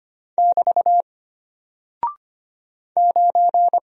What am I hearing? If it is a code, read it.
Morse code, heard as a keyed tone, about 25 wpm, 700 Hz. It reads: <BT> 9
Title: Single Character (Letter:Number:Punctuation:Prosign) Alt 25wpm